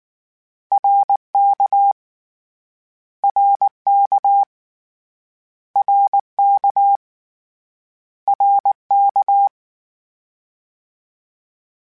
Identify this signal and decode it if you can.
Morse code, heard as a keyed tone, about 19 wpm, 800 Hz. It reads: RK RK RK RK
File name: Short Runs / Short Runs-mp3s / ShortContrasts @18